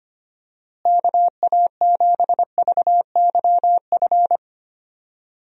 Morse code, heard as a keyed tone, about 25 wpm, 700 Hz. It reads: KA7VYF
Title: USA Call Signs - 6 Characters - Repeated 3x - 25wpm